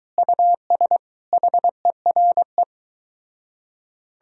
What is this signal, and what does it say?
Morse code, heard as a keyed tone, about 23 wpm, 700 Hz. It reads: US HERE